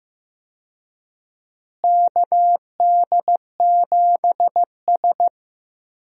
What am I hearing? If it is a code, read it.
Morse code, heard as a keyed tone, about 15 wpm, 700 Hz. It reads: KD7S
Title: USA Call Signs - 4 Characters - Repeated 3x - 15wpm